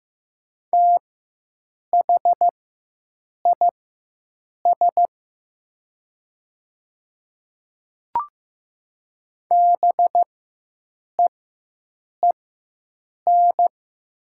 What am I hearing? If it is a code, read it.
Morse code, heard as a keyed tone, about 15 wpm, spaced as for 7 wpm, 700 Hz. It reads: THIS BEEN
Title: 4x Character Spacing - Top 100 Words - 15wpm